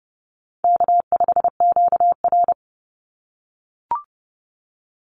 Morse code, heard as a keyed tone, about 30 wpm, 700 Hz. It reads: K5QR
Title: NA SOTA Chaser Callsigns 30wpm